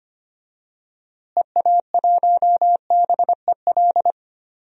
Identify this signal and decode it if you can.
Morse code, heard as a keyed tone, about 25 wpm, 700 Hz. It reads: EA1BEL
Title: DXCC-100 Call Signs - 6 Characters - Repeated 2x - 25wpm